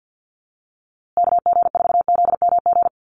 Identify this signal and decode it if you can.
Morse code, heard as a keyed tone, about 50 wpm, 700 Hz. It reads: KG4ZMG
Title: USA Call Signs Repeated 2x 50wpm